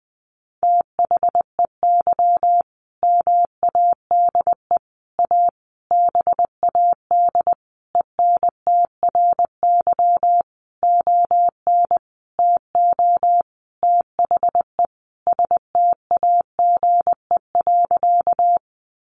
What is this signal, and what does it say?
Morse code, heard as a keyed tone, about 20 wpm, 700 Hz. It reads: THEY MADE A BAD ENTRY ON TO THE STAGE.
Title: Sentences from Top 1500 Words 20wpm